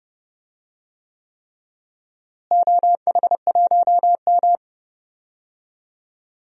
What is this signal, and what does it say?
Morse code, heard as a keyed tone, about 30 wpm, 700 Hz. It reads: OH1M
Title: DXCC-100 Call Signs - 4 Characters - Repeated 3x - 30wpm